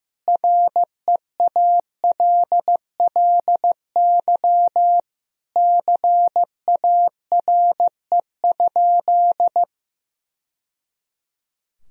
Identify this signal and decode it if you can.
Morse code, heard as a keyed tone, about 15 wpm, 700 Hz. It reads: REALLY CARE?
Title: Sentences from Top 1000 Words - Review Part 3 - 15wpm